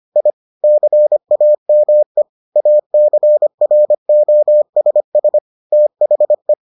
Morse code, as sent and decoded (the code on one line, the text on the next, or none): .. / -.-. .- -- . / .- -.-. .-. --- ... ... / - .... .
I CAME ACROSS THE